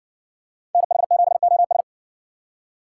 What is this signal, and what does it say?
Morse code, heard as a keyed tone, about 60 wpm, 700 Hz. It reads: NH7OS